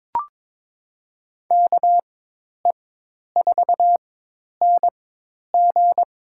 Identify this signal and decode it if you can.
Morse code, heard as a keyed tone, about 22 wpm, spaced as for 10 wpm, 700 Hz. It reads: KE4NG
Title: USA Call Signs - 4x Character Spacing - 22wpm